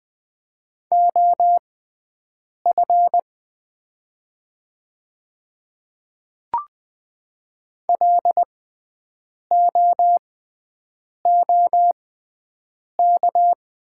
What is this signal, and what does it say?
Morse code, heard as a keyed tone, about 20 wpm, spaced as for 7 wpm, 700 Hz. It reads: OF LOOK
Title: 6x Character Spacing - Top 100 Words - 20wpm